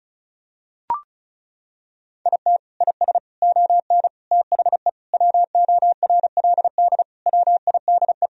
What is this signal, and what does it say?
Morse code, heard as a keyed tone, about 36 wpm, 700 Hz. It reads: IT IS ON THE WORLD WIDE
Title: Sentences from Top 1000 Words - Review Part 4 - 35wpm